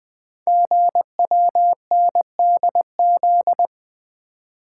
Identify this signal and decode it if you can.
Morse code, heard as a keyed tone, about 20 wpm, 700 Hz. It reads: GWNDZ
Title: Code Groups - 5 Letters - 20wpm